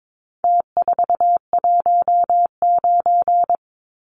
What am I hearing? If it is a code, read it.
Morse code, heard as a keyed tone, about 22 wpm, 700 Hz. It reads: T419
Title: Numbers - Contest Style - 4-Digit 22wpm